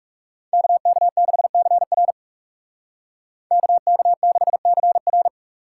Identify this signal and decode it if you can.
Morse code, heard as a keyed tone, about 36 wpm, 700 Hz. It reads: KK6CR KK6CR
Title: Speed Racing - USA 2x2 Call Signs 30wpm